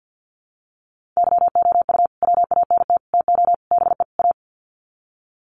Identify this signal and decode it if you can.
Morse code, heard as a keyed tone, about 48 wpm, 700 Hz. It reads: YOU WANT TO BE A
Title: Sets of 5 Words Encore2 50wpm